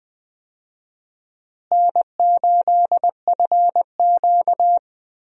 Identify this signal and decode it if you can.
Morse code, heard as a keyed tone, about 20 wpm, 700 Hz. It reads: N8FQ
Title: USA Call Signs - 4 Characters - Repeated 3x - 20wpm